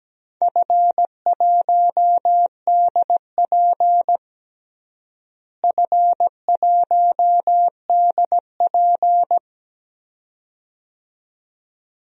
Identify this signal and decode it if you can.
Morse code, heard as a keyed tone, about 17 wpm, 700 Hz. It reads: F1DP F1DP
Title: DXCC-100 Call Signs - 4 Characters - Repeated 2x - 17wpm